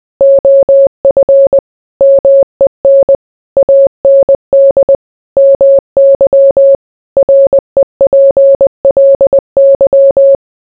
Morse code, heard as a keyed tone, about 20 wpm, 550 Hz. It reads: OF MEN AND MY REPLY